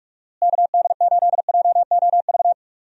Morse code, heard as a keyed tone, about 46 wpm, 700 Hz. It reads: KD8JOV
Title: USA Call Signs - 6 Characters - Repeated 3x - 45wpm